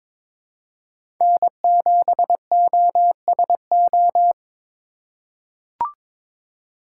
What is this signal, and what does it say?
Morse code, heard as a keyed tone, about 22 wpm, 700 Hz. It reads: N7OSO
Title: USA Call Signs - 5 Characters - 4x Character Spacing - 22wpm